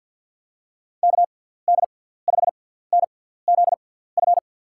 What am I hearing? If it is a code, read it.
Morse code, heard as a keyed tone, about 53 wpm, spaced as for 17 wpm, 700 Hz. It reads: KD5NZF
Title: USA Call Signs - 6 Characters - 6x Character Spacing - 50wpm